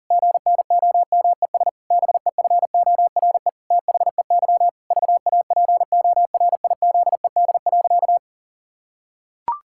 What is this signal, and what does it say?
Morse code, heard as a keyed tone, about 40 wpm, 700 Hz. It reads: GNOMES BEFORE THEY VAPORIZED.
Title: Pangram Challenge v2 - 40wpm